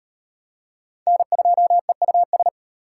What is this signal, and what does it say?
Morse code, heard as a keyed tone, about 32 wpm, 700 Hz. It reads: N2EUS